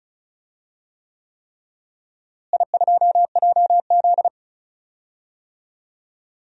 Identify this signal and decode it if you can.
Morse code, heard as a keyed tone, about 36 wpm, 700 Hz. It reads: I2JZ
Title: DXCC-100 Call Signs - 4 Characters - Repeated 3x - 35wpm